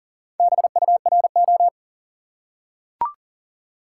Morse code, heard as a keyed tone, about 40 wpm, 700 Hz. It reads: BURO
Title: Top 180 QSO Elements 40wpm